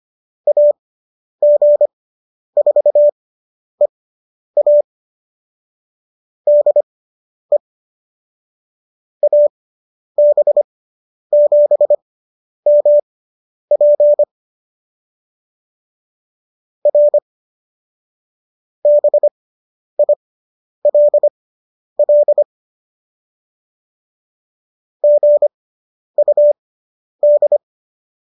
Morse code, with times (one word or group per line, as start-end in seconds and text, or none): 0.5-4.8 AG4EA
6.5-7.6 DE
9.2-14.2 AB7MP
16.9-17.2 R
18.9-22.4 BILL
25.0-27.6 GUD